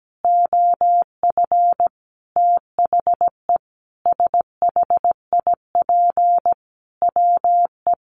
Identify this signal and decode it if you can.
Morse code, heard as a keyed tone, about 17 wpm, 700 Hz. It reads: OF THE SHIP WE
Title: Sentences from Top 800 Words 17wpm